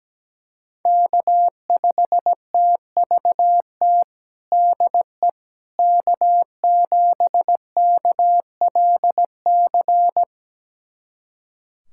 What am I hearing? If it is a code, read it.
Morse code, heard as a keyed tone, about 17 wpm, 700 Hz. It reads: K5TVT DE K7KLC
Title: USA Callsign DE Callsign - 5 Characters - 17wpm